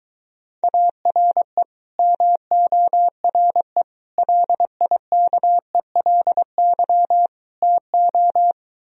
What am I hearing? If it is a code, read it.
Morse code, heard as a keyed tone, about 23 wpm, 700 Hz. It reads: ARE MORE LIKELY TO